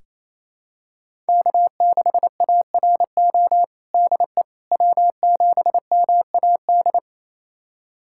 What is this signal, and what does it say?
Morse code, heard as a keyed tone, about 28 wpm, 700 Hz. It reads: K6ARO DE W7MAD
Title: USA Callsign DE Callsign - 5 Characters - 28wpm